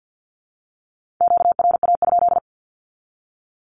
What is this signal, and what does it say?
Morse code, heard as a keyed tone, about 50 wpm, 700 Hz. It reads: QRA?
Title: Q-Codes with Spoken Meaning - 50wpm